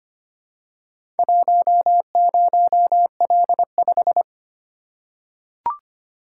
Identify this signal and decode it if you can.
Morse code, heard as a keyed tone, about 25 wpm, 700 Hz. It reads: 10L5
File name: practice / 25wpm / Four Letter-Number Combinations - 25wpm